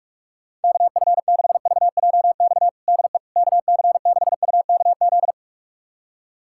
Morse code, heard as a keyed tone, about 41 wpm, 700 Hz. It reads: KF6VJX DE KC6UKZ
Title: USA Callsign DE Callsign - 6 Characters - 45wpm